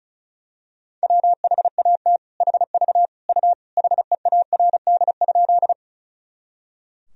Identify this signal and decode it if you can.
Morse code, heard as a keyed tone, about 36 wpm, 700 Hz. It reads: WHAT HV U HEARD?
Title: Sentences from Top 400 Words and Abbreviations 35wpm